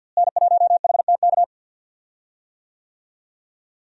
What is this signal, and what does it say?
Morse code, heard as a keyed tone, about 49 wpm, 700 Hz. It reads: N1HTK
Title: USA Call Signs - 5 Characters - Repeated 3x - 50wpm